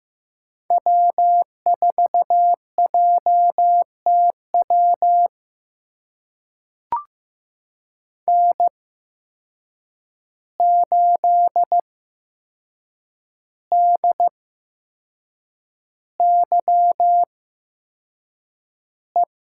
Morse code, heard as a keyed tone, about 15 wpm, spaced as for 6 wpm, 700 Hz. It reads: W4JTW N8DYE
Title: USA Call Signs - 5 Characters - 8x Character Spacing - 15wpm